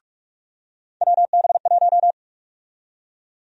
Morse code, heard as a keyed tone, about 46 wpm, 700 Hz. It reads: WB1
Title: USA Call Sign Prefixes - 45wpm